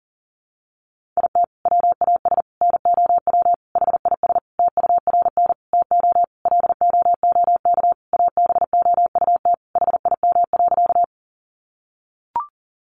Morse code, heard as a keyed tone, about 40 wpm, 700 Hz. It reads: IT WAS NOW HIS TURN TO LOOK ABOUT HIM.
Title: Sentences from Top 200 Words 40wpm